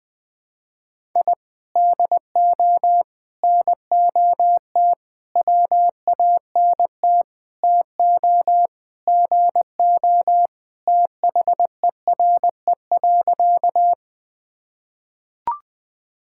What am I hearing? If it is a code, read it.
Morse code, heard as a keyed tone, about 21 wpm, 700 Hz. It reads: I DO NOT WANT TO GO THERE.
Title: Sentences with 4-Word Phrases - Spoken Beginning - 20wpm